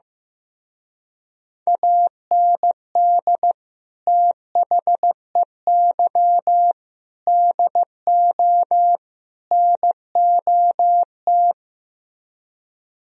Morse code, heard as a keyed tone, about 15 wpm, 700 Hz. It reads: AND THEY DO NOT
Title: Speed-Racing - Sets of 4 Words 15wpm